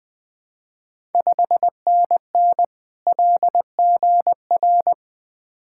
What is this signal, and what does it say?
Morse code, heard as a keyed tone, about 20 wpm, 700 Hz. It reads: 5NN LGR
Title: Ontario QSO Party Exchange 20wpm